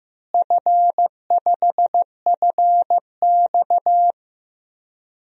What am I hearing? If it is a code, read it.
Morse code, heard as a keyed tone, about 16 wpm, 700 Hz. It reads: F5FX
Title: DXCC-100 Call Signs - 4 Characters - Repeated 3x - 15wpm